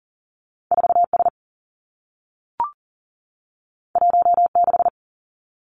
Morse code, heard as a keyed tone, about 40 wpm, 700 Hz. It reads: 4 S 1 6
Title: Two Challenging Letter-Number Combinations 40wpm